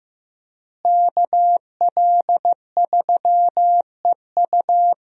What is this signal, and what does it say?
Morse code, heard as a keyed tone, about 15 wpm, 700 Hz. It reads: KL3EU
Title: USA Call Signs - 5 Characters - Repeated 3x - 15wpm